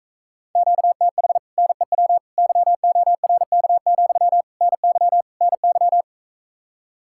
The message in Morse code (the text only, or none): QTH NEW YORK, NY NY